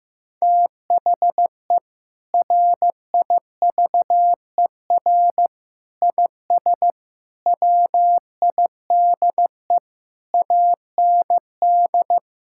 Morse code, heard as a keyed tone, about 16 wpm, 700 Hz. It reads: THE RIVER IS WIDE AND
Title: Sentences from Top 1000 Words - Review Part 6 - 15wpm